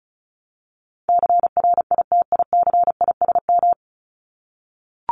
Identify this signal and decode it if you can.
Morse code, heard as a keyed tone, about 34 wpm, 700 Hz. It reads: CRITICISM
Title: Top 1001-1250 Words 35wpm